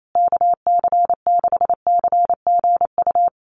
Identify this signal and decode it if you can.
Morse code, heard as a keyed tone, about 28 wpm, 700 Hz. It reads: KC6CGU
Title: USA Call Signs Repeated 2x 28wpm